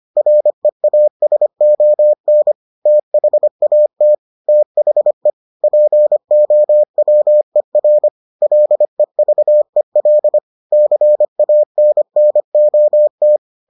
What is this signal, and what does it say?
Morse code, heard as a keyed tone, about 25 wpm, 600 Hz. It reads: REASON THAT THE POWER LEVEL CANNOT